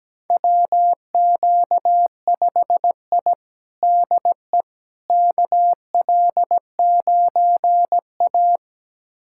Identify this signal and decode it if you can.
Morse code, heard as a keyed tone, about 17 wpm, 700 Hz. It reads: WQ5I DE KL9A
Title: USA Callsign DE Callsign - 4 Characters - 17wpm